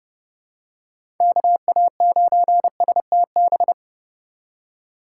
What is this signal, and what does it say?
Morse code, heard as a keyed tone, about 32 wpm, 700 Hz. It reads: KA9STB
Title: USA Call Signs - 6 Characters - 4x Character Spacing - 30wpm